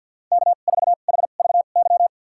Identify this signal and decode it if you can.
Morse code, heard as a keyed tone, about 51 wpm, spaced as for 36 wpm, 700 Hz. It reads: K4HVY
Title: USA Call Signs - 5 Characters - 2x Character Spacing - 50wpm